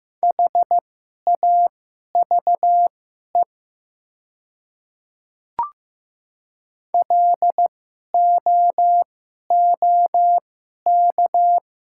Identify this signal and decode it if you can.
Morse code, heard as a keyed tone, about 15 wpm, spaced as for 11 wpm, 700 Hz. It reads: HAVE LOOK
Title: 2x Character Spacing - Top 100 Words - 15wpm